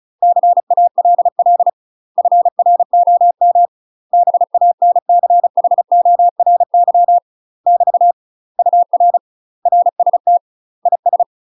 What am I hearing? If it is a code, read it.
Morse code, heard as a keyed tone, about 35 wpm, 700 Hz. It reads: CALL FROM BANCHORY <BT> UR RST IS
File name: UK MASTER QSOs TRAINING / 35WPM / UK MASTER QSOs 0013